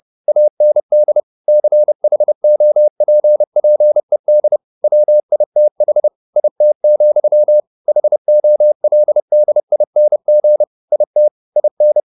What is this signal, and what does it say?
Morse code, heard as a keyed tone, about 30 wpm, 600 Hz. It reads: AND CHOPPED WITH IT, HOLDING IT IN